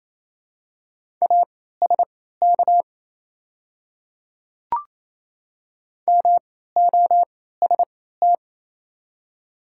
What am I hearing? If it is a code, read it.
Morse code, heard as a keyed tone, about 27 wpm, spaced as for 16 wpm, 690 Hz. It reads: ASK MOST
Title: 3x Character Spacing - Top 200 Words - 28wpm